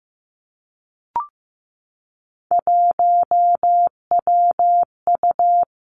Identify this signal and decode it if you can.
Morse code, heard as a keyed tone, about 15 wpm, 700 Hz. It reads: 1WU
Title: Three Challenging Letter-Number Combinations 15wpm